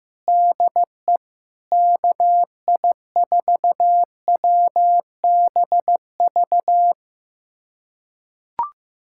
Morse code, heard as a keyed tone, about 16 wpm, 700 Hz. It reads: DE KI4WBV
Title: USA Callsign DE Callsign - 6 Characters - 15wpm